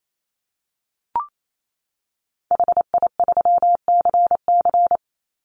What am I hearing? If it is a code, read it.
Morse code, heard as a keyed tone, about 28 wpm, 700 Hz. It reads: HI3CC
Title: Call Signs Repeated 3x - 28wpm